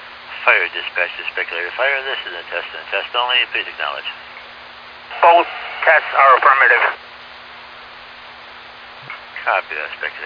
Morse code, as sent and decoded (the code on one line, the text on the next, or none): none